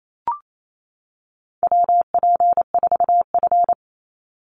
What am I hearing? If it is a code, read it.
Morse code, heard as a keyed tone, about 28 wpm, 700 Hz. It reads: WP4F
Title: Call Signs - 28wpm